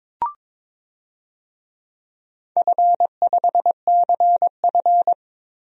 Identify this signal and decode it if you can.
Morse code, heard as a keyed tone, about 22 wpm, 700 Hz. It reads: F5CF
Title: DXCC-100 Call Signs - 4 Characters - 22wpm